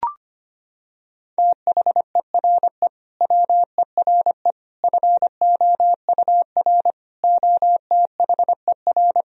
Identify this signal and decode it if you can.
Morse code, heard as a keyed tone, about 25 wpm, 700 Hz. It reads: THERE WERE FOUR OTHER
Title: Sentences from Top 1000 Words - Review Part 1 - 25wpm